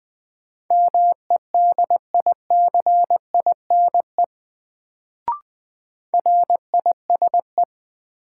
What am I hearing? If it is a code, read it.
Morse code, heard as a keyed tone, about 20 wpm, 700 Hz. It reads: MEDICINE RISE
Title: Sentences from Top 900 Words - New Word Drill - 20wpm